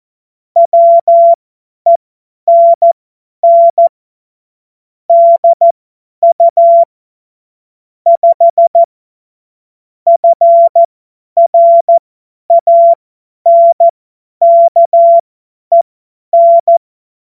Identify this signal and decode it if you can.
Morse code, heard as a keyed tone, about 14 wpm, spaced as for 10 wpm, 680 Hz. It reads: WENN DU 5 FRANKEN
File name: Morsekurs / Witze / Witz 10. August 2021, 50 F70